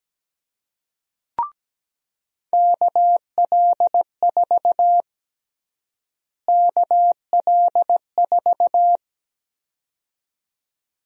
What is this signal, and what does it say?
Morse code, heard as a keyed tone, about 17 wpm, 700 Hz. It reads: KL4 KL4
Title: USA Call Sign Prefixes - Repeated 2x - 17wpm